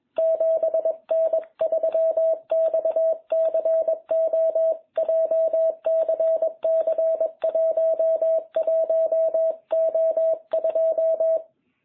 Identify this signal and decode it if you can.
Morse code, heard as a keyed tone, about 21 wpm, 660 Hz. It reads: 7N3XCO JCC 11O2